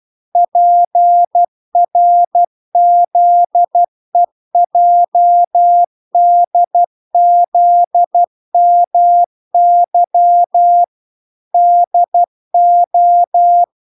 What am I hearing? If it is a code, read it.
Morse code, heard as a keyed tone, about 12 wpm, 700 Hz. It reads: PRZEJDZMY DO